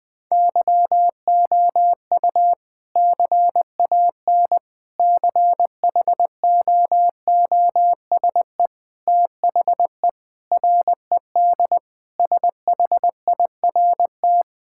Morse code, heard as a keyed tone, about 21 wpm, 700 Hz. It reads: YOU CAN CHOOSE THE RED SHIRT
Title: Sentences with 2-Word Phrases - 20wpm